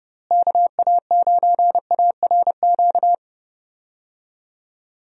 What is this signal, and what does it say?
Morse code, heard as a keyed tone, about 30 wpm, 700 Hz. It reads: KA9ARQ
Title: USA Call Signs - 6 Characters - Repeated 2x - 30wpm